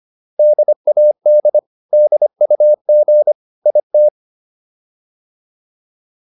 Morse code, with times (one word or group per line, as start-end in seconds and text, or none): 0.4-1.6 DAD
1.9-3.3 DUG
3.7-4.1 IT